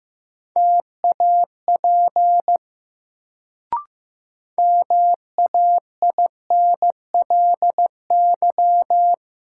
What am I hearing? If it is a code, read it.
Morse code, heard as a keyed tone, about 15 wpm, 700 Hz. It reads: TAP MAINLY
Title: Sentences from Top 1400 Words - New Word Drill - 15wpm